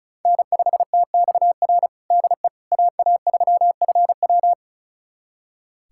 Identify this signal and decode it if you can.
Morse code, heard as a keyed tone, about 34 wpm, 690 Hz. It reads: N5TXR DE AA3FW
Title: USA Callsign DE Callsign - 5 Characters - 35wpm